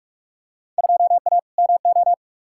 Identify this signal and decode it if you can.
Morse code, heard as a keyed tone, about 41 wpm, 700 Hz. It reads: 2A MO
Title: Field Day Exchanges Encore - 45wpm